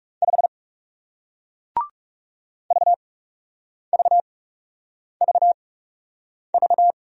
Morse code, heard as a keyed tone, about 40 wpm, spaced as for 8 wpm, 700 Hz. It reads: 5 VVVV